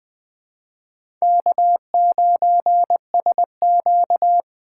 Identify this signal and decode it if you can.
Morse code, heard as a keyed tone, about 20 wpm, 700 Hz. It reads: K9SQ
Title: USA Call Signs - 4 Characters - 2x Character Spacing - 20wpm